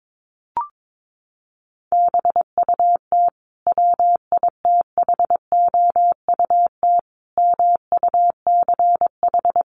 Morse code, heard as a keyed tone, about 22 wpm, 700 Hz. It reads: BUT WITHOUT MUCH
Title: Sentences from Top 900 Words 22wpm